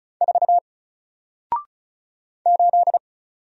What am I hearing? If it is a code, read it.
Morse code, heard as a keyed tone, about 33 wpm, 700 Hz. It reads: 4 8
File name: Single-Letter-Number-35wpm